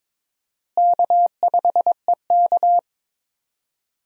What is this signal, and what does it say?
Morse code, heard as a keyed tone, about 22 wpm, 700 Hz. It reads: K5EK
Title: USA Call Signs - 4 Characters - Repeated 3x - 22wpm